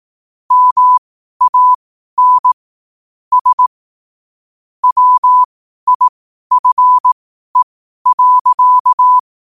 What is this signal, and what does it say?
Morse code, heard as a keyed tone, about 18 wpm, spaced as for 12 wpm, 1000 Hz. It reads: MAN S WIFE.
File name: qotdfiles / 12wpm